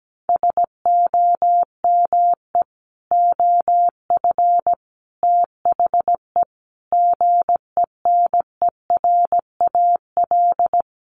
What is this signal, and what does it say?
Morse code, heard as a keyed tone, about 17 wpm, 700 Hz. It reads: SOME OF THE GENERAL
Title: Sentences from Top 800 Words 17wpm